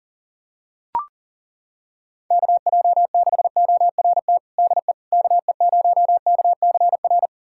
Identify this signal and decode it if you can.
Morse code, heard as a keyed tone, about 40 wpm, 700 Hz. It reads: KJ6ORT DE KE0KCR
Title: USA Callsign DE Callsign - 6 Characters - 40wpm